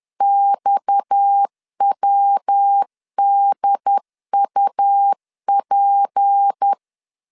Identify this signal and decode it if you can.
Morse code, heard as a keyed tone, about 10 wpm, 800 Hz. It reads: XWDUP
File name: LEZIONI / 10 SIGNS / 10D COMA 10wpm 3013